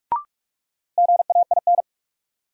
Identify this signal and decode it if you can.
Morse code, heard as a keyed tone, about 45 wpm, 700 Hz. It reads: GAIN
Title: Sentences from Top 1000 Words - New Word Drill - 45wpm